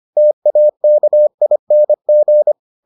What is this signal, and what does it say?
Morse code, heard as a keyed tone, about 25 wpm, 600 Hz. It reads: TAKING